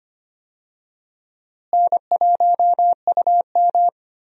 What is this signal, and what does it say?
Morse code, heard as a keyed tone, about 25 wpm, 700 Hz. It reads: N1UM
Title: USA Call Signs - 4 Characters - Repeated 2x - 25wpm